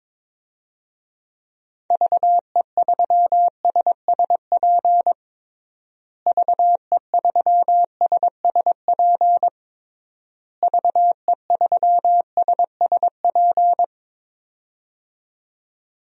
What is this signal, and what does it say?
Morse code, heard as a keyed tone, about 23 wpm, 700 Hz. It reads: VE3SSP VE3SSP VE3SSP
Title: DXCC-100 Call Signs - 6 Characters - Repeated 3x - 22wpm